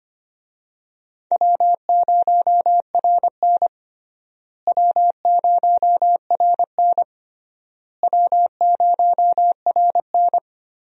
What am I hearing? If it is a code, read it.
Morse code, heard as a keyed tone, about 25 wpm, 700 Hz. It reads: W0RN W0RN W0RN
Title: USA Call Signs - 4 Characters - Repeated 3x - 25wpm